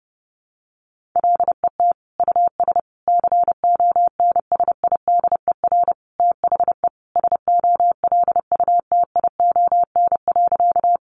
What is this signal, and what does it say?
Morse code, heard as a keyed tone, about 30 wpm, 700 Hz. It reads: LET US CONSIDER THE SOLUTION.
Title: Sentences from Top 1300 Words 30wpm